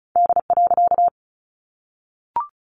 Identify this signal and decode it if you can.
Morse code, heard as a keyed tone, about 34 wpm, 700 Hz. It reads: D.
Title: A Sound Of Thunder 35wpm